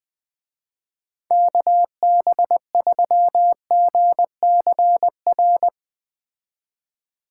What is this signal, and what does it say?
Morse code, heard as a keyed tone, about 20 wpm, 700 Hz. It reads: KB3GCR
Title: Speed Racing - USA 2x3 Call Signs 17wpm